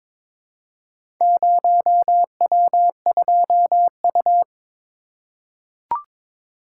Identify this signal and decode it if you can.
Morse code, heard as a keyed tone, about 22 wpm, 700 Hz. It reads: 0W2U
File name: practice / 22wpm / Four Letter-Number Combinations - 22wpm